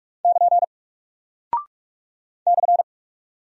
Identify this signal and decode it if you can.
Morse code, heard as a keyed tone, about 44 wpm, 700 Hz. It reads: <KN> /
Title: Single Character (Letter:Number:Punctuation:Prosign) Alt 45wpm